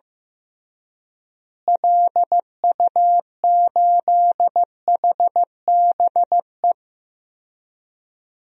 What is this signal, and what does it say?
Morse code, heard as a keyed tone, about 15 wpm, 700 Hz. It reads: LU8HBE
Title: DXCC-100 Call Signs - 6 Characters - Repeated 3x - 15wpm